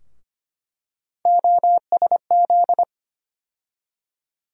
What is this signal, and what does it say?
Morse code, heard as a keyed tone, about 25 wpm, 700 Hz. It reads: OSZ
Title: USA Call Sign Postfixes - 3 Letters 25wpm